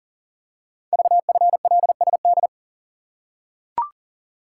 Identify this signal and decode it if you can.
Morse code, heard as a keyed tone, about 43 wpm, 700 Hz. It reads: VFLSD
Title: Code Groups - 5 Letters - 40wpm